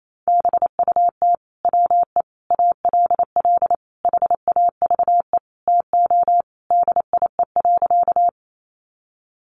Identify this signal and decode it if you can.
Morse code, heard as a keyed tone, about 28 wpm, 700 Hz. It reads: BUT WE ALL HAVE TO DIE.
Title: Sentences from Top 1300 Words 28wpm